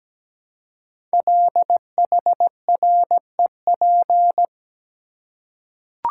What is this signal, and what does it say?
Morse code, heard as a keyed tone, about 17 wpm, 700 Hz. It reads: LHREP
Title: Code Groups - 5 Letters - 17wpm